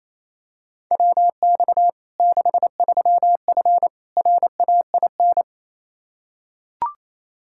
Morse code, heard as a keyed tone, about 28 wpm, 700 Hz. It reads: WX 63F RAIN
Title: Practice QSO - Exchange 2 - Info - 28wpm